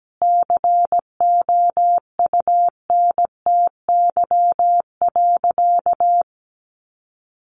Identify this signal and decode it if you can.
Morse code, heard as a keyed tone, about 17 wpm, 700 Hz. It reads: COUNTY.
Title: Sentences from Top 1200 Words 17wpm